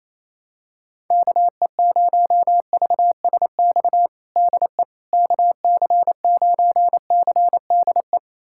Morse code, heard as a keyed tone, about 28 wpm, 700 Hz. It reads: KE0VSX DE KC9CDE
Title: USA Callsign DE Callsign 28wpm